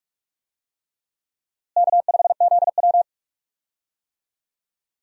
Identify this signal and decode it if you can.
Morse code, heard as a keyed tone, about 46 wpm, 700 Hz. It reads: K5ZW
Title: USA Call Signs - 4 Characters - Repeated 3x - 45wpm